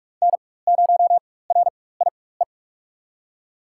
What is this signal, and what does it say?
Morse code, heard as a keyed tone, about 42 wpm, spaced as for 21 wpm, 700 Hz. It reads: N0RIE
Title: USA Call Signs - 5 Characters - 4x Character Spacing - 45wpm